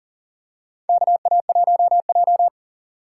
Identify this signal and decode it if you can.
Morse code, heard as a keyed tone, about 39 wpm, 700 Hz. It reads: KA1J
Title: USA Call Signs - 4 Characters - 6x Character Spacing - 40wpm